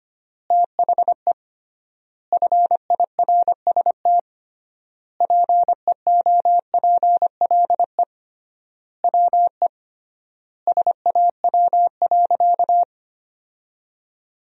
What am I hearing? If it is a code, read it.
Morse code, heard as a keyed tone, about 25 wpm, 700 Hz. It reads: THE FIRST PEOPLE WE SAW.
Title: 3x Word-Spacing - Sentences from Top 300 Words - 25wpm